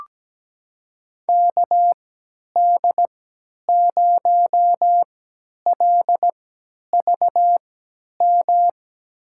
Morse code, heard as a keyed tone, about 17 wpm, spaced as for 10 wpm, 700 Hz. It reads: KD0LVM
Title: USA Call Signs - 6 Characters - 3x Character Spacing - 17wpm